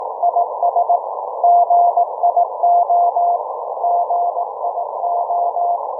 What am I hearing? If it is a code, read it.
Morse code, heard as a keyed tone, about 18 wpm, 690 Hz, with noise. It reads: IS GIO GIO